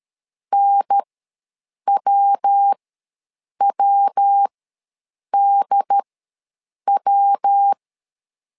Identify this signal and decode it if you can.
Morse code, heard as a keyed tone, about 13 wpm, spaced as for 7 wpm, 800 Hz. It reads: NWWDW
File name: LEZIONI / 3 MWRO / 3 MW ws 12wpm 1816